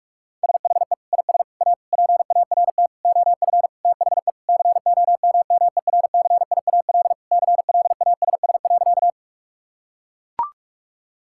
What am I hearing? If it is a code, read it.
Morse code, heard as a keyed tone, about 42 wpm, 700 Hz. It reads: SHE IS A PART OF THE COMMERCIAL CLASS.
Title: Sentences from Top 1000 Words - Review Part 5 - 45wpm